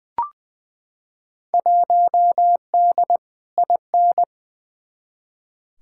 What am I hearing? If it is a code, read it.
Morse code, heard as a keyed tone, about 20 wpm, 700 Hz. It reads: 1D IN
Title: Field Day Exchanges - 20wpm